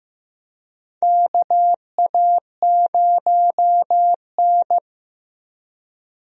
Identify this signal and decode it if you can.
Morse code, heard as a keyed tone, about 15 wpm, 700 Hz. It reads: KA0N
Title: USA Call Signs - 4 Characters - Repeated 3x - 15wpm